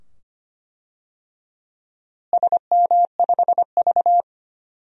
Morse code, heard as a keyed tone, about 25 wpm, 700 Hz. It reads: SM5V
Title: DXCC-100 Call Signs - 4 Characters - Repeated 2x - 25wpm